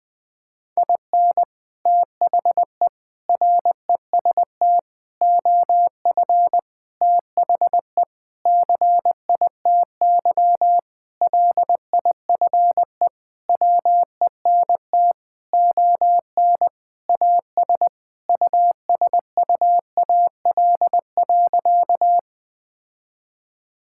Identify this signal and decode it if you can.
Morse code, heard as a keyed tone, about 20 wpm, 700 Hz. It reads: IN THE REST OF THE CITY LIFE WENT ON AS USUAL.
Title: Sentences with 5-Word Phrases - Spoken Beginning - 20wpm